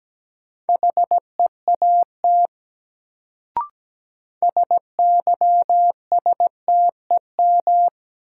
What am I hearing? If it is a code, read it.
Morse code, heard as a keyed tone, about 17 wpm, 700 Hz. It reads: HEAT SYSTEM
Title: Sentences from Top 1000 Words - Review Part 3 - New Word Drill - 17wpm